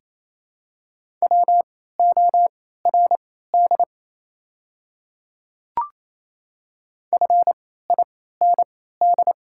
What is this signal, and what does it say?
Morse code, heard as a keyed tone, about 27 wpm, spaced as for 16 wpm, 690 Hz. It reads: WORD FIND
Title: 3x Character Spacing - Top 100 Words - 28wpm